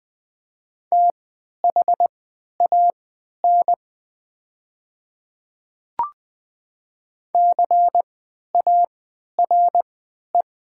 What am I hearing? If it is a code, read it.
Morse code, heard as a keyed tone, about 20 wpm, spaced as for 11 wpm, 700 Hz. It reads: THAN CARE
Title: 3x Character Spacing - Top 300 Words - 20wpm